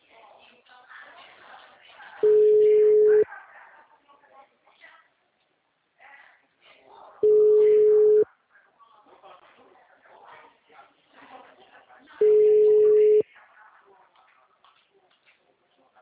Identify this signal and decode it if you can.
Ringback tone (Europe, 425 Hz), with speech.
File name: out- -2005- - - .36093